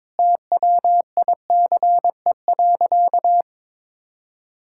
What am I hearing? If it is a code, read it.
Morse code, heard as a keyed tone, about 22 wpm, 700 Hz. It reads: TWICE.
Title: Sentences from Top 1600 Words 22wpm